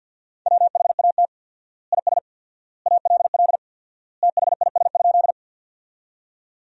Morse code, heard as a keyed tone, about 49 wpm, 700 Hz. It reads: WHAT IS ALL THIS?
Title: 4x Word-Spacing - Sentences from Top 100 Words - 50wpm